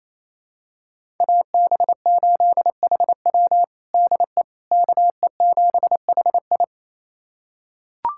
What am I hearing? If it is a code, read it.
Morse code, heard as a keyed tone, about 27 wpm, 700 Hz. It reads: AB8HW DE KE7HI
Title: USA Callsign DE Callsign - 5 Characters - 28wpm